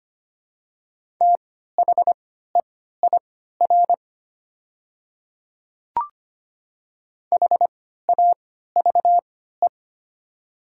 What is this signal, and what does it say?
Morse code, heard as a keyed tone, about 25 wpm, spaced as for 14 wpm, 690 Hz. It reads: THEIR HAVE
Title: 3x Character Spacing - Top 100 Words - 25wpm